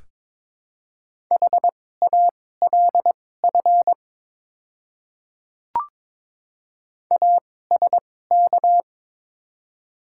Morse code, heard as a keyed tone, about 22 wpm, spaced as for 16 wpm, 700 Hz. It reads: HALF ASK
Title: 2x Character Spacing - Top 500 Words - 22wpm